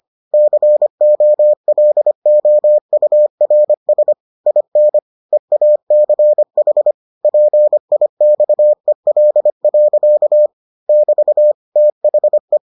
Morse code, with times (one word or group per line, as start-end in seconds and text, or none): 0.3-4.1 COLOURS
4.5-5.0 IN
5.3-6.9 EACH
7.2-10.5 PIXEL.
10.9-11.5 <BT>
11.8-12.6 THE